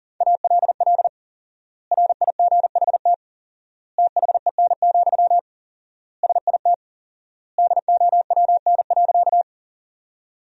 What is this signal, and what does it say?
Morse code, heard as a keyed tone, about 40 wpm, 700 Hz. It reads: ALL RIGHT THEN, SIT DOWN.
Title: 4x Word-Spacing - Sentences from Top 500 Words - 40wpm